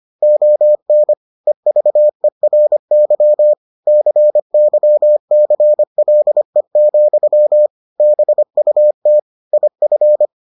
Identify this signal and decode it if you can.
Morse code, heard as a keyed tone, about 25 wpm, 600 Hz. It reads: ON EVERY CYCLE, BUT IF